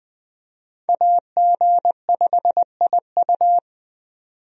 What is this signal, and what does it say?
Morse code, heard as a keyed tone, about 21 wpm, 700 Hz. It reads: AG5IU
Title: USA Call Signs - 5 Characters - 20wpm